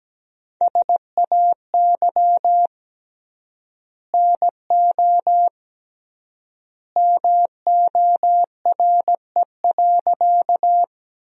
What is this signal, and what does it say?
Morse code, heard as a keyed tone, about 17 wpm, 700 Hz. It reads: SAY NO MORE.
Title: 3x Word-Spacing - Sentences from Top 200 Words - 17wpm